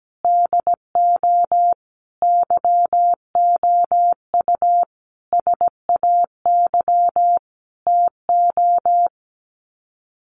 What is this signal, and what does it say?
Morse code, heard as a keyed tone, about 17 wpm, 700 Hz. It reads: DO YOU SAY TO
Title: Sets of 5 Words 17wpm